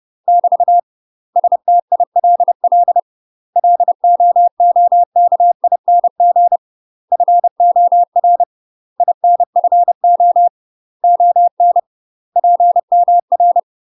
Morse code, heard as a keyed tone, about 30 wpm, 700 Hz. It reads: <BT> STILL LOOKING FOR INFO ON PMR
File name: UK MASTER QSOs TRAINING/30WPM